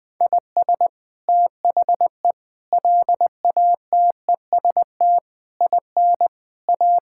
Morse code, heard as a keyed tone, about 20 wpm, 700 Hz. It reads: IS THE LATEST IN A
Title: Sets of 8 Words - 20wpm